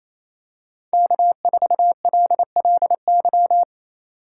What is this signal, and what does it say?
Morse code, heard as a keyed tone, about 28 wpm, 700 Hz. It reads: K4LLY
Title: USA Call Signs - 5 Characters - 2x Character Spacing - 28wpm